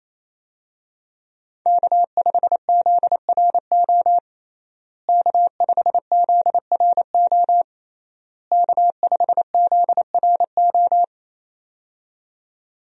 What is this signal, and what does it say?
Morse code, heard as a keyed tone, about 28 wpm, 700 Hz. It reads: K5ZRO K5ZRO K5ZRO
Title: USA Call Signs - 5 Characters - Repeated 3x - 28wpm